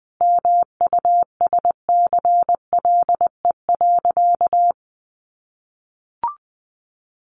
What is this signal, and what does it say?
Morse code, heard as a keyed tone, about 20 wpm, 700 Hz. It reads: MUSCLE.
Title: Sentences from Top 1300 Words 20wpm